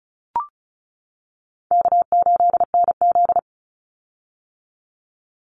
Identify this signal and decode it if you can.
Morse code, heard as a keyed tone, about 34 wpm, 700 Hz. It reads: K8NZ
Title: Call Signs - 35wpm